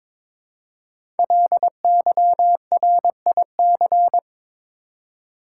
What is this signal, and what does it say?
Morse code, heard as a keyed tone, about 23 wpm, 700 Hz. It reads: LYRIC
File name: 5 Letter Words - First 3 Letters Given - 22wpm